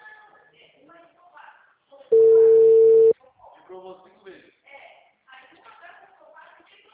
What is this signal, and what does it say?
Ringback tone (China, 450 Hz), with speech.